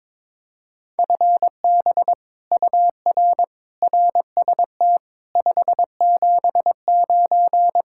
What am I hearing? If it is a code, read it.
Morse code, heard as a keyed tone, about 22 wpm, 700 Hz. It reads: FB UR RST 579
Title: Speed-Racing Common QSO Phrases - 22wpm